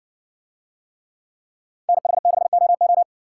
Morse code, heard as a keyed tone, about 59 wpm, 700 Hz. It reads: NH6OO